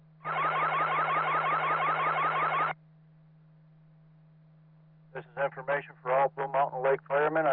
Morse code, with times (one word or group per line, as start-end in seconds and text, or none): none